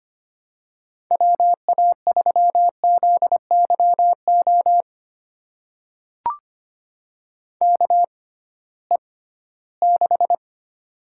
Morse code, heard as a keyed tone, about 25 wpm, spaced as for 16 wpm, 700 Hz. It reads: WA3ZYO KE6
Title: USA Call Signs - 6x Character Spacing - 25wpm